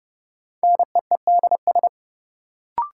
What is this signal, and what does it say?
Morse code, heard as a keyed tone, about 30 wpm, 700 Hz. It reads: NEEDS
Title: Sets of 3 Words - New Word Drill - 30wpm